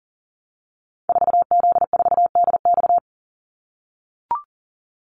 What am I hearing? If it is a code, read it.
Morse code, heard as a keyed tone, about 41 wpm, 700 Hz. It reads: 4Z4DX
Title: Call Signs - 40wpm